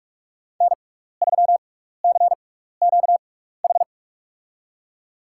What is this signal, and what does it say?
Morse code, heard as a keyed tone, about 46 wpm, spaced as for 15 wpm, 700 Hz. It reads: N3CQH
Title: USA Call Signs - 6x Character Spacing - 45wpm